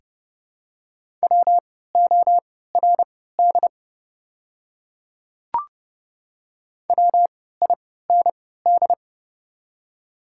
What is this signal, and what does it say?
Morse code, heard as a keyed tone, about 30 wpm, spaced as for 17 wpm, 700 Hz. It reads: WORD WIND
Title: 3x Character Spacing - Top 500 Words - 30wpm